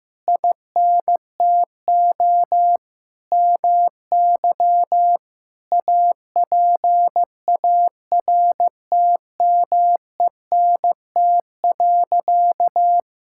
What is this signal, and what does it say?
Morse code, heard as a keyed tone, about 16 wpm, 700 Hz. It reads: INTO MY APARTMENT.
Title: Sentences from Top 1000 Words - Review Part 6 - 15wpm